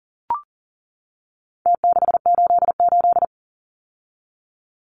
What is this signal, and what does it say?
Morse code, heard as a keyed tone, about 39 wpm, 700 Hz. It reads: T688
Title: Numbers - Contest Style - 4-Digit 40wpm